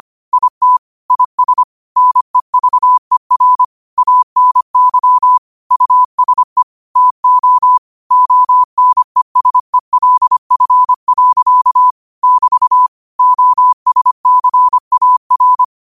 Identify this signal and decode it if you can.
Morse code, heard as a keyed tone, about 25 wpm, 1000 Hz. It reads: IT IS NEVER ANY USE TO ONESELF. <BT> OSCAR
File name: qotdfiles / 25wpm